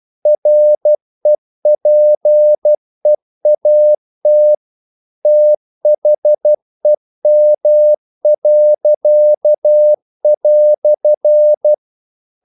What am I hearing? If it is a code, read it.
Morse code, heard as a keyed tone, about 12 wpm, 600 Hz. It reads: REPEAT THEM."
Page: English quotations in CW (telegraphy - morse)